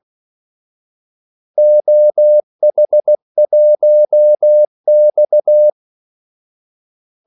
Morse code, heard as a keyed tone, about 16 wpm, 600 Hz. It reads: OH1X
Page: Kaldesignaler 16 wpm